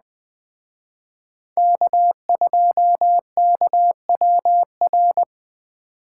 Morse code, heard as a keyed tone, about 21 wpm, 700 Hz. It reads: K2KWR
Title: Speed Racing - USA 1x3 Call Signs 17wpm